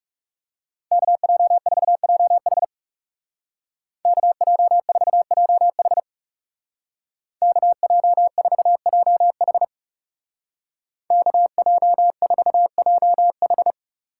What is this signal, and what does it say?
Morse code, heard as a keyed tone, about 39 wpm, 700 Hz. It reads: KJ4JH KJ4JH KJ4JH KJ4JH